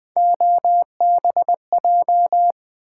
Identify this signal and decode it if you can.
Morse code, heard as a keyed tone, about 20 wpm, 700 Hz. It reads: OBJ
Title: Build A Word - 3 Letters At A Time - 9-Letter Words - 20wpm